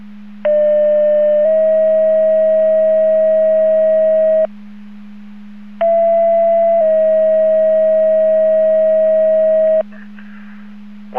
Two-tone page, heard as a one-second tone, then a three-second tone, about 600 then 640 Hz, about 670 then 640 Hz. It is heard from 0.5 to 9.8 s.
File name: Portage 154.385 Paging